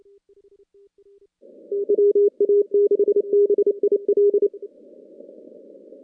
Morse code, heard as a keyed tone, about 28 wpm, 410 Hz. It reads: WA6BIL
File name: contests / CW-SS-k5tr- -2100